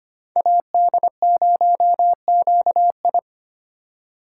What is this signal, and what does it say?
Morse code, heard as a keyed tone, about 25 wpm, 700 Hz. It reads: AD0QI
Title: USA Call Signs - 8x Character Spacing - 25wpm